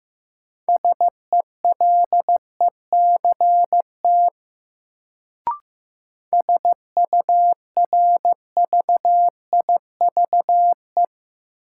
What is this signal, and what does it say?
Morse code, heard as a keyed tone, about 15 wpm, 700 Hz. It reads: SELECT SURVIVE
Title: Sentences from Top 1300 Words - New Word Drill - 15wpm